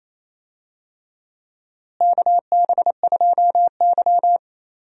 Morse code, heard as a keyed tone, about 27 wpm, 700 Hz. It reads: KB2Y
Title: USA Call Signs - 4 Characters - Repeated 3x - 28wpm